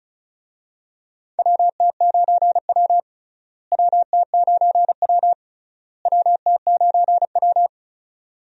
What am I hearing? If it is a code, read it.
Morse code, heard as a keyed tone, about 36 wpm, 700 Hz. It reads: WT9W WT9W WT9W
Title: USA Call Signs - 4 Characters - Repeated 3x - 35wpm